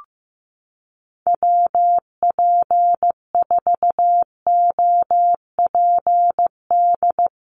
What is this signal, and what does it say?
Morse code, heard as a keyed tone, about 14 wpm, 700 Hz. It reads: WP4OPD
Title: USA Call Signs Repeated 2x 15wpm